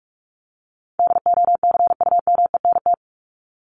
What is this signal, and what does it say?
Morse code, heard as a keyed tone, about 44 wpm, 700 Hz. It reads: DOCUMENT
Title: Top 1751-2000 Words 45wpm